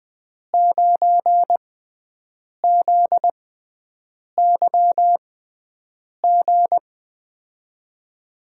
Morse code, heard as a keyed tone, about 20 wpm, spaced as for 7 wpm, 700 Hz. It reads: 9ZYG
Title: USA Call Signs - 5 Characters - 6x Character Spacing - 20wpm